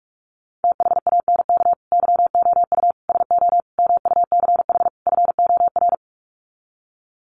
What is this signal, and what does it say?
Morse code, heard as a keyed tone, about 44 wpm, 700 Hz. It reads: THANK YOU SO MUCH FOR
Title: Sets of 5 Words 45wpm